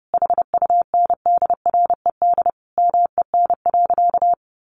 Morse code, heard as a keyed tone, about 30 wpm, 700 Hz. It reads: HUNDRED MEN.
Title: Sentences from Top 1400 Words 30wpm